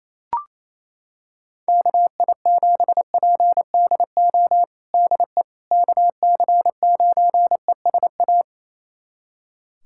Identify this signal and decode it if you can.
Morse code, heard as a keyed tone, about 28 wpm, 700 Hz. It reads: KI7PDO DE KC9ESA
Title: USA Callsign DE Callsign - 6 Characters - 28wpm